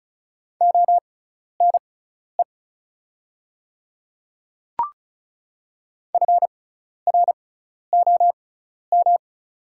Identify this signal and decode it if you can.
Morse code, heard as a keyed tone, about 34 wpm, spaced as for 12 wpm, 700 Hz. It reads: ONE FROM
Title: 6x Character Spacing - Top 100 Words - 35wpm